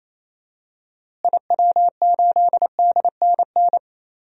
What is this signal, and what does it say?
Morse code, heard as a keyed tone, about 28 wpm, 700 Hz. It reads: IW8DNN
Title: DXCC-100 Call Signs - 6 Characters - Repeated 3x - 28wpm